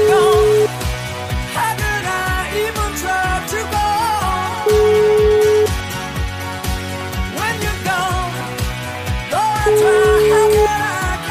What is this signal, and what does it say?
Ringback tone (Europe, 425 Hz), with noise.